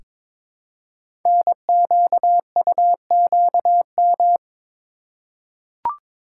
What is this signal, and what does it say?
Morse code, heard as a keyed tone, about 22 wpm, 700 Hz. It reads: NQUQM
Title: Code Groups - 5 Letters - 22wpm